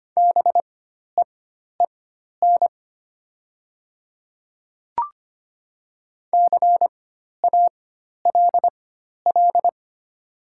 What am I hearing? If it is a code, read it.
Morse code, heard as a keyed tone, about 25 wpm, spaced as for 12 wpm, 700 Hz. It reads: BEEN CALL
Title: 4x Character Spacing - Top 100 Words - 25wpm